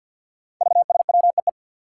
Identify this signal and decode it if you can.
Morse code, heard as a keyed tone, about 51 wpm, 700 Hz. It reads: VSWEE
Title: Five Challenging Letter-Number Combinations 50wpm